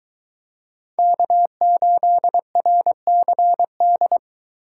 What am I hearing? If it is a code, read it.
Morse code, heard as a keyed tone, about 21 wpm, 700 Hz. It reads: K8RCD